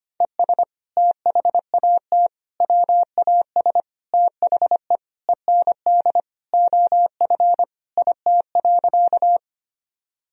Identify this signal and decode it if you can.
Morse code, heard as a keyed tone, about 25 wpm, 700 Hz. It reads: ES THAT WAS THE END OF IT.
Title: Sentences from Top 200 Words and Abbreviations 25wpm